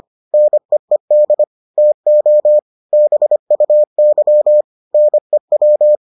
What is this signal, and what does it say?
Morse code, heard as a keyed tone, about 25 wpm, 600 Hz. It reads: NEED TO BUY NEW